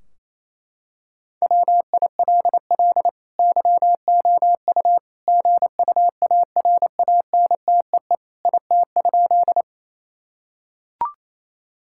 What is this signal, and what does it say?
Morse code, heard as a keyed tone, about 27 wpm, 700 Hz. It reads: WILL YOU GUARANTEE IT?
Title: Sentences from Top 1600 Words 28wpm